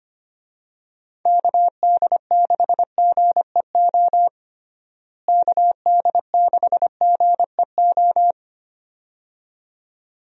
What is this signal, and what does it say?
Morse code, heard as a keyed tone, about 25 wpm, 700 Hz. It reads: KD6GEO KD6GEO
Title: USA Call Signs - 6 Characters - Repeated 2x - 25wpm